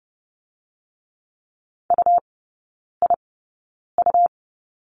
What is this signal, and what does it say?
Morse code, heard as a keyed tone, about 30 wpm, spaced as for 9 wpm, 700 Hz. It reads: UIU
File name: Lesson 17e - Letter u - Three Characters 30wpm